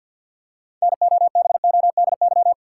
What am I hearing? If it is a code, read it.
Morse code, heard as a keyed tone, about 49 wpm, 700 Hz. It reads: NOBODY
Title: Six Letter Words - 50wpm